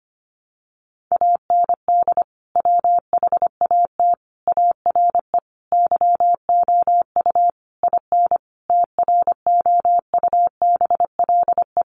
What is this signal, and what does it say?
Morse code, heard as a keyed tone, about 25 wpm, 700 Hz. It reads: AND WHAT ARE YOU IN TROUBLE
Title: Sentences from Top 1200 Words 25wpm